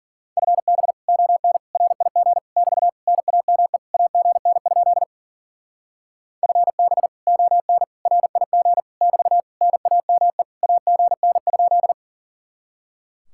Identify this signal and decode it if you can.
Morse code, heard as a keyed tone, about 50 wpm, 700 Hz. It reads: FB ON RIG <BT> NAME AGN? FB ON RIG <BT> NAME AGN?
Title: Speed-Racing Common QSO Phrases Encore - 40wpm